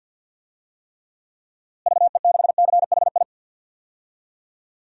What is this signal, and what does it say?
Morse code, heard as a keyed tone, about 52 wpm, 700 Hz. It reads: VE6CHI
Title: DXCC-100 Call Signs - 6 Characters - Repeated 2x - 50wpm